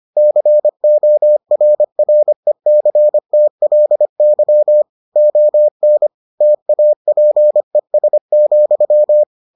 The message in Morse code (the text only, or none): CORRECTLY ON TAPES,